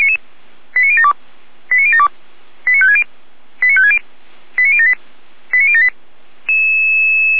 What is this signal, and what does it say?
Five-tone selective call (ZVEI), heard as quick runs of five tones, tones of about 70 ms. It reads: 89072 89072 89560 89560 89078 89078